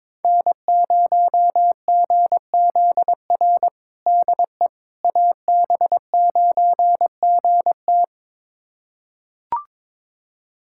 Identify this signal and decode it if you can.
Morse code, heard as a keyed tone, about 22 wpm, 700 Hz. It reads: N0GZR DE AB9GT
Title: USA Callsign DE Callsign - 5 Characters - 22wpm